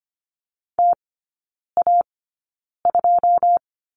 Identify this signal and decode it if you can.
Morse code, heard as a keyed tone, about 25 wpm, spaced as for 9 wpm, 700 Hz. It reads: TA2
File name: Lesson 15e - Number 5 - Three Characters 25wpm